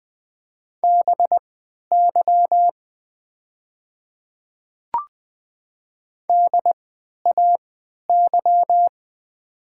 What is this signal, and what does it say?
Morse code, heard as a keyed tone, about 20 wpm, spaced as for 11 wpm, 700 Hz. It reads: BY DAY
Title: 3x Character Spacing - Top 100 Words - 20wpm